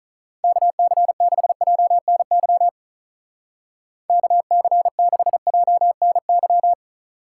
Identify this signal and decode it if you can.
Morse code, heard as a keyed tone, about 43 wpm, 700 Hz. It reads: KC6JNY KC6JNY
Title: Speed Racing - USA 2x3 Call Signs 35wpm